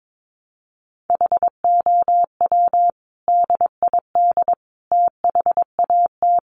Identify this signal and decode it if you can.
Morse code, heard as a keyed tone, about 22 wpm, 700 Hz. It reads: HOW DID THAT
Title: Sentences from Top 200 Words 22wpm